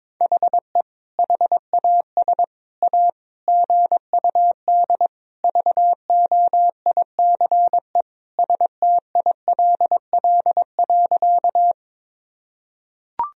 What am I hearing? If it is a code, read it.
Morse code, heard as a keyed tone, about 22 wpm, 700 Hz. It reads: HE HAS A GUD VOICE STILL.
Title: Sentences from Top 500 Words and Abbreviations 22wpm